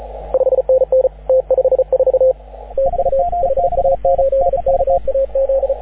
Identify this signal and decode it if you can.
Morse code, heard as a keyed tone, about 34 wpm, 530 Hz, with radio noise. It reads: 5NN T54 TU HA8KAZ
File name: AUDIO2019 / HG6N CW SSB MIX